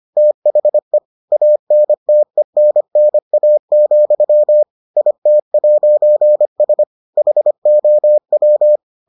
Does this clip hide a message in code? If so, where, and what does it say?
Yes: Morse code from 0.2 to 8.8 s, heard as a keyed tone, about 25 wpm, 600 Hz. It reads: THE ANTENNA, IT'S HOW